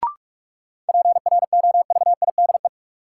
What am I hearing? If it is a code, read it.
Morse code, heard as a keyed tone, about 44 wpm, 700 Hz. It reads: PROVIDE
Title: Sentences from Top 1100 Words - New Word Drill Alt - 45wpm